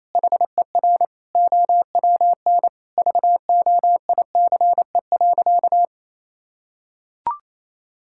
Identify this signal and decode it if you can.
Morse code, heard as a keyed tone, about 28 wpm, 700 Hz. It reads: HER OWN VOICE.
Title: Sentences from Top 1600 Words 28wpm